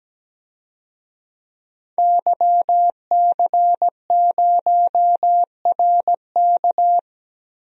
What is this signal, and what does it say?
Morse code, heard as a keyed tone, about 17 wpm, 700 Hz. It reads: YC0RK
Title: DXCC-100 Call Signs - 5 Characters - 17wpm